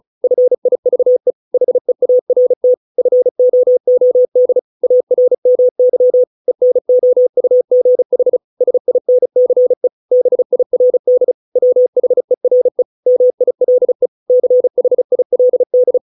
Morse code, heard as a keyed tone, about 34 wpm, 500 Hz. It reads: FIVE HEART FOOD ARMY ENOUGH SINCE BIRD WHERE MILE CHILD